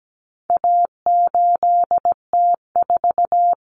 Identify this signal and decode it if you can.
Morse code, heard as a keyed tone, about 17 wpm, 700 Hz. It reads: A8T4
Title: Common Cut Numbers - 4-Digit 17wpm